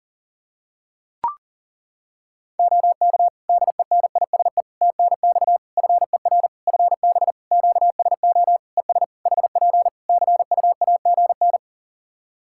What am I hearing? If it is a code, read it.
Morse code, heard as a keyed tone, about 37 wpm, 700 Hz. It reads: OK DENISE TNX FER FB QSO ES HP CUAGN
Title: K1USN SST QSO Practice 40wpm